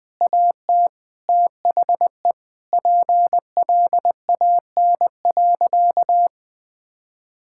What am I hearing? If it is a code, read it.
Morse code, heard as a keyed tone, about 21 wpm, 700 Hz. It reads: AT THE PLAN.
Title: Sentences with 6-Word Phrases - 20wpm